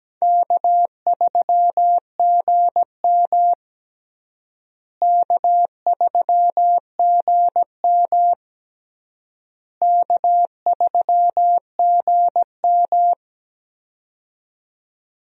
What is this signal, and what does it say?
Morse code, heard as a keyed tone, about 17 wpm, 700 Hz. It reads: K3GM K3GM K3GM
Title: USA Call Signs - 4 Characters - Repeated 3x - 17wpm